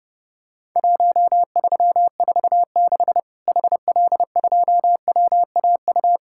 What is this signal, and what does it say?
Morse code, heard as a keyed tone, about 30 wpm, 700 Hz. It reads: 1346 HL2WAU
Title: Worked All Europe DX Contest - QTC Exchange - 30wpm